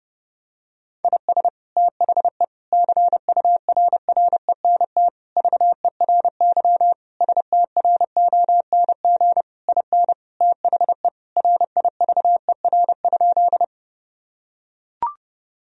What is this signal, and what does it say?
Morse code, heard as a keyed tone, about 30 wpm, 700 Hz. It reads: IS THE CURRENT VERY STRONG IN THE RIVER?
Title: Sentences from Top 1000 Words - Review Part 4 - 30wpm